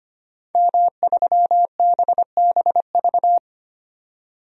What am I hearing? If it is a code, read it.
Morse code, heard as a keyed tone, about 25 wpm, 700 Hz. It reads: M3BBV
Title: DXCC-100 Call Signs - 5 Characters - 25wpm